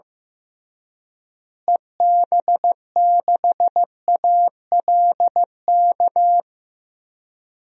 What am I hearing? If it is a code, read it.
Morse code, heard as a keyed tone, about 16 wpm, 700 Hz. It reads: EB6ALK
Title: DXCC-100 Call Signs - 6 Characters - Repeated 3x - 15wpm